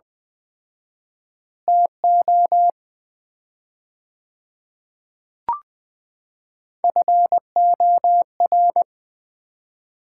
Morse code, heard as a keyed tone, about 20 wpm, 700 Hz. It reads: TO FOR
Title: 4x Word-Spacing - Sets of 4 Words - 20wpm